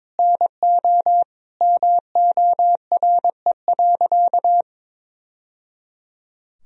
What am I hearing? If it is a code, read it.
Morse code, heard as a keyed tone, about 22 wpm, 700 Hz. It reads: NO MORE.
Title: Sentences from Top 1000 Words - Review Part 1 - 22wpm